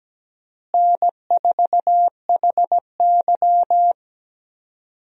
Morse code, heard as a keyed tone, about 17 wpm, 700 Hz. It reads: N4HY
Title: USA Call Signs - 4 Characters - Repeated 2x - 17wpm